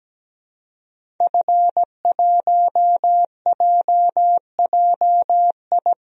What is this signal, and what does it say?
Morse code, heard as a keyed tone, about 17 wpm, 700 Hz. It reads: F1JJI
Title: DXCC-100 Call Signs - 5 Characters - Repeated 3x - 17wpm